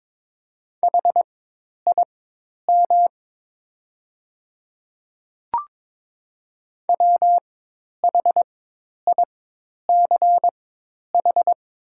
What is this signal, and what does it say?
Morse code, heard as a keyed tone, about 22 wpm, spaced as for 10 wpm, 700 Hz. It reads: HIM WHICH
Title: 4x Character Spacing - Top 100 Words - 22wpm